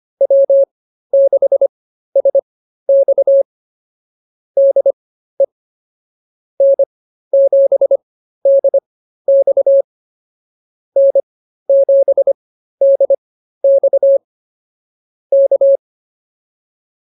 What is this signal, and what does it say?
Morse code, heard as a keyed tone, about 24 wpm, spaced as for 13 wpm, 560 Hz. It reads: W6SX DE N7DX N7DX K